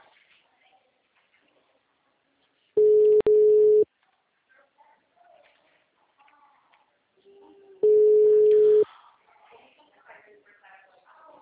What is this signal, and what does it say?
Ringback tone (Europe, 425 Hz), with speech.